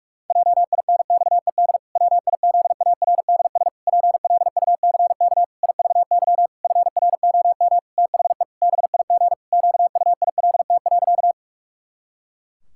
Morse code, heard as a keyed tone, about 41 wpm, 700 Hz. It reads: JINXED WIZARDS PLUCK IVY FROM THE BIG QUILT.
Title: Pangram Challenge v2 - 45wpm